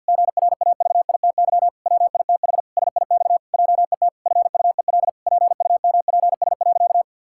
Morse code, heard as a keyed tone, about 51 wpm, 700 Hz. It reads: GRAVITY WITH SIX JET FUEL PUMPS.
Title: Pangram Challenge v2 - 50wpm